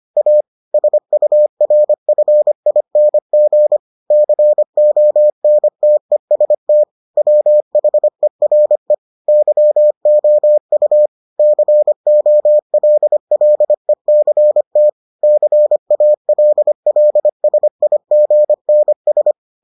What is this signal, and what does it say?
Morse code, heard as a keyed tone, about 25 wpm, 600 Hz. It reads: A SURFING CONTEST WHERE YOU COLLECT CALLSIGNS